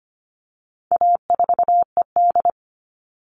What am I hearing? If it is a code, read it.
Morse code, heard as a keyed tone, about 25 wpm, 700 Hz. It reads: A4ED
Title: All Cut Numbers - 4-Digit 25wpm